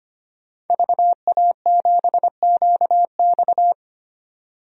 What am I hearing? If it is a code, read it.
Morse code, heard as a keyed tone, about 25 wpm, 700 Hz. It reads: VA7QX
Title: DXCC-100 Call Signs - 5 Characters - 25wpm